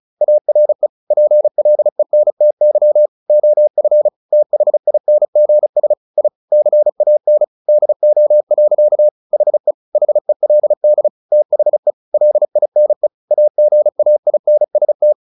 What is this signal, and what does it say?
Morse code, heard as a keyed tone, about 34 wpm, 600 Hz. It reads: ARE PLENTY OF THINGS I CAN DO. HE HELD THE LINE AGAINST
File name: the old man and the sea / the old man and the sea 1 35wpm